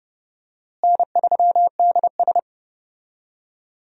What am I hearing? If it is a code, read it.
Morse code, heard as a keyed tone, about 31 wpm, 700 Hz. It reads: N3DS
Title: USA Call Signs - 4 Characters - Repeated 2x - 30wpm